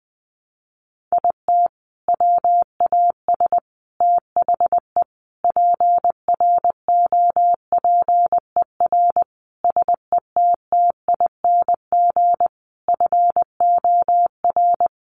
Morse code, heard as a keyed tone, about 20 wpm, 700 Hz. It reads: IT WAS THE PROPER SETTING FOR
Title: Sentences from Top 1300 Words 20wpm